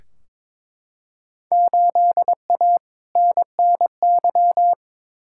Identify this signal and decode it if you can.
Morse code, heard as a keyed tone, about 22 wpm, 700 Hz. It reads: 8A NNY
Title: Field Day Exchanges Encore - 22wpm